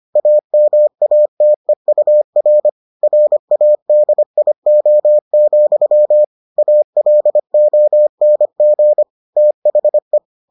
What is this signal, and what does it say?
Morse code, heard as a keyed tone, about 25 wpm, 600 Hz. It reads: AMATEUR RADIO, ALONG THE